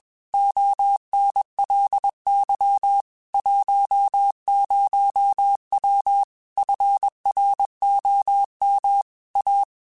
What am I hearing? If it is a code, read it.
Morse code, heard as a keyed tone, about 21 wpm, 800 Hz. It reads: ONLY 10W FROM A